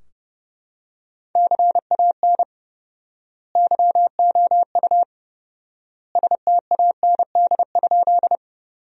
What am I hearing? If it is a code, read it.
Morse code, heard as a keyed tone, about 30 wpm, 700 Hz. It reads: CAN YOU STAND?
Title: 4x Word-Spacing - Sentences from Top 300 Words - 30wpm